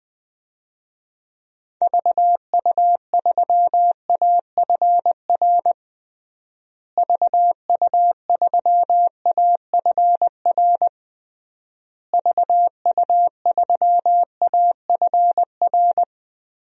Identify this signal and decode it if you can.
Morse code, heard as a keyed tone, about 21 wpm, 700 Hz. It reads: VU3AFR VU3AFR VU3AFR
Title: DXCC-100 Call Signs - 6 Characters - Repeated 3x - 20wpm